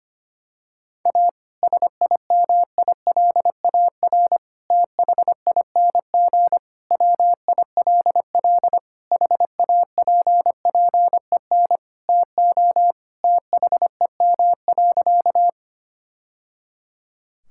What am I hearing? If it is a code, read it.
Morse code, heard as a keyed tone, about 25 wpm, 700 Hz. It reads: A SIMILAR THING WILL HAPPEN TO THEM.
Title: Sentences from Top 1000 Words - Review Part 5 - 25wpm